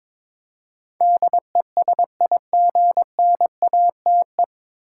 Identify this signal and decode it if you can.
Morse code, heard as a keyed tone, about 22 wpm, 700 Hz. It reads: DESIGNATE
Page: Morse Code Ninja practice 22wpm